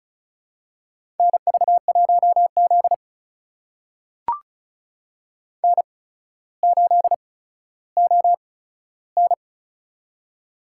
Morse code, heard as a keyed tone, about 34 wpm, spaced as for 15 wpm, 700 Hz. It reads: NV1Z N8ON
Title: USA Call Signs - 4 Characters - 8x Character Spacing - 35wpm